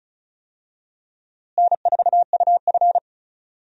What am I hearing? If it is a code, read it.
Morse code, heard as a keyed tone, about 35 wpm, 700 Hz. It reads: N4UF
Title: USA Call Signs - 4 Characters - Repeated 2x - 35wpm